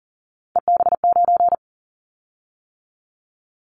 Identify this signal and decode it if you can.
Morse code, heard as a keyed tone, about 40 wpm, 700 Hz. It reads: EB9
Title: Callsign Prefixes for Top 100 DXCC - 40wpm